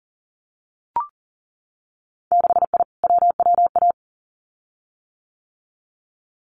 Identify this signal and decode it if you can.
Morse code, heard as a keyed tone, about 42 wpm, 700 Hz. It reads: 6I WWA
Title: Winter Field Day Exchanges Encore 40wpm